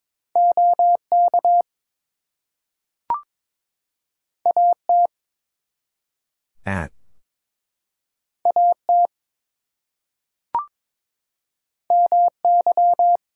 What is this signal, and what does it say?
Morse code, heard as a keyed tone, about 22 wpm, 700 Hz. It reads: OK AT AT MY
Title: Two Letter Words - 22wpm